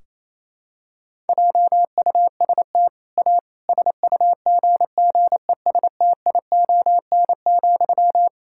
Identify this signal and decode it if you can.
Morse code, heard as a keyed tone, about 28 wpm, 700 Hz. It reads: JUST A SUGGESTION,
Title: Sentences from Top 1600 Words 28wpm